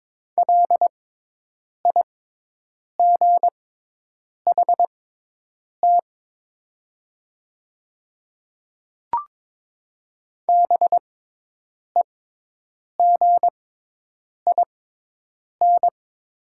Morse code, heard as a keyed tone, about 22 wpm, spaced as for 8 wpm, 700 Hz. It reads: LIGHT BEGIN
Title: 6x Character Spacing - Top 300 Words - 22wpm